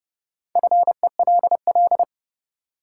Morse code, heard as a keyed tone, about 30 wpm, 700 Hz. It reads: FELL
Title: Sentences from Top 1600 Words - New Word Drill Alt - 30wpm